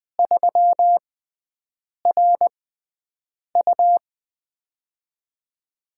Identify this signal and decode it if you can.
Morse code, heard as a keyed tone, about 20 wpm, spaced as for 7 wpm, 700 Hz. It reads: 3RU
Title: USA Call Signs - 4 Characters - 6x Character Spacing - 20wpm